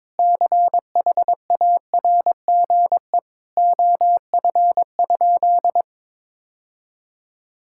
Morse code, heard as a keyed tone, about 22 wpm, 700 Hz. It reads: CHARGE OF?
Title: Sentences from Top 1000 Words - Review Part 6 - 22wpm